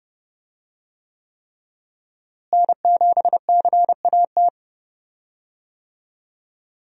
Morse code, heard as a keyed tone, about 29 wpm, 700 Hz. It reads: N7CAT
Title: USA Call Signs - 5 Characters - Repeated 3x - 30wpm